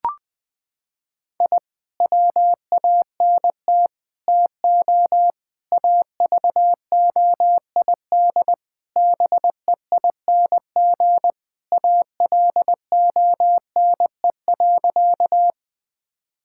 Morse code, heard as a keyed tone, about 20 wpm, 700 Hz. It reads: I WANT TO AVOID BEING ALONE.
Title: Sentences from Top 1000 Words - Review Part 6 - 20wpm